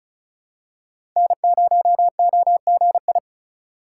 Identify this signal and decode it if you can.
Morse code, heard as a keyed tone, about 36 wpm, 700 Hz. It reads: N0OGI
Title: USA Call Signs - 5 Characters - 4x Character Spacing - 35wpm